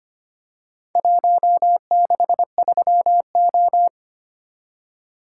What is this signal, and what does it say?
Morse code, heard as a keyed tone, about 25 wpm, 700 Hz. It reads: 163O
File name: practice / 25wpm / Four Letter-Number Combinations - 25wpm